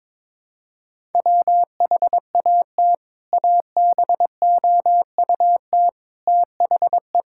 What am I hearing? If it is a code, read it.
Morse code, heard as a keyed tone, about 22 wpm, 700 Hz. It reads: WHAT ABOUT THE
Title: Sentences from Top 1600 Words 22wpm